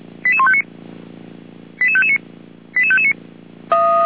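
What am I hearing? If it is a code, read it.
Five-tone selective call (ZVEI), heard as quick runs of five tones, tones of about 70 ms. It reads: 88159 88559 88559
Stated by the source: ZVEI five-tone selective call received via SDR.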